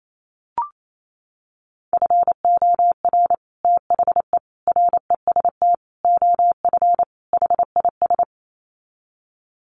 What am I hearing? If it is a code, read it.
Morse code, heard as a keyed tone, about 28 wpm, 700 Hz. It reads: FOR THE REST OF HIS
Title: Sets of 5 Words 28wpm